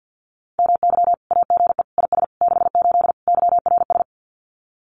Morse code, heard as a keyed tone, about 50 wpm, 700 Hz. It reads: MY AGE IS 68 YRS
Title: Common QSO Phrases Encore 50wpm